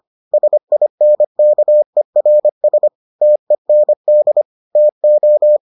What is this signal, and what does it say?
Morse code, heard as a keyed tone, about 25 wpm, 600 Hz. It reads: SINKERS TEND TO